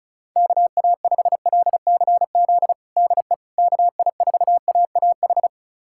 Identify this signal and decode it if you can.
Morse code, heard as a keyed tone, about 36 wpm, 700 Hz. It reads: KA5LCZ DE KI4AAH
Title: USA Callsign DE Callsign - 6 Characters - 35wpm